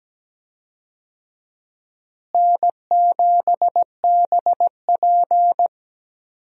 Morse code, heard as a keyed tone, about 17 wpm, 700 Hz. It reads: N7BP
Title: USA Call Signs - 4 Characters - Repeated 2x - 17wpm